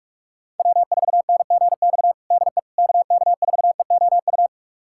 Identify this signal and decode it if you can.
Morse code, heard as a keyed tone, about 46 wpm, 700 Hz. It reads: W4NGX DE KK4EOU
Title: USA Callsign DE Callsign 45wpm